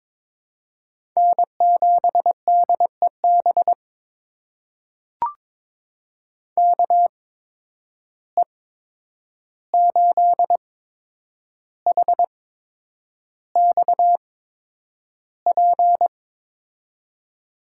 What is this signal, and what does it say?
Morse code, heard as a keyed tone, about 21 wpm, spaced as for 9 wpm, 700 Hz. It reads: N7DEB KE8HXP
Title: USA Call Signs - 8x Character Spacing - 22wpm